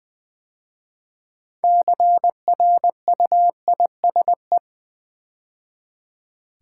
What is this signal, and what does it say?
Morse code, heard as a keyed tone, about 20 wpm, 700 Hz. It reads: CRUISE
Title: 6 Letter Words - First 3 Letters Given - 20wpm